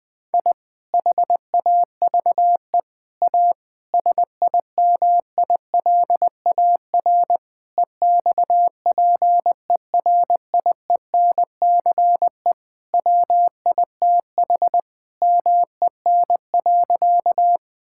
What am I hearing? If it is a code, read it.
Morse code, heard as a keyed tone, about 20 wpm, 700 Hz. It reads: I HAVE A SIMILAR EXPERIENCE WITH MEN.
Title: Sentences from Top 1000 Words - Review Part 5 - 20wpm